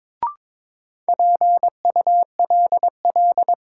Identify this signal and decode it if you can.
Morse code, heard as a keyed tone, about 23 wpm, 700 Hz. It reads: PULL
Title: Sentences from Top 400 Words - New Word Drill - 22wpm